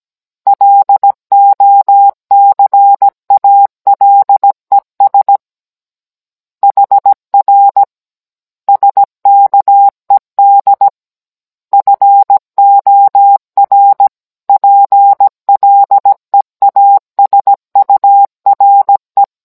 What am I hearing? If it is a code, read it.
Morse code, heard as a keyed tone, about 17 wpm, 800 Hz. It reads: LOCALES HR SKED FOR PLEASURE